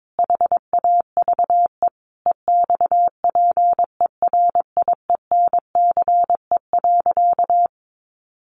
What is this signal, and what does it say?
Morse code, heard as a keyed tone, about 22 wpm, 700 Hz. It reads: HAVE EXPERIENCE.
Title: Sentences from Top 700 Words 22wpm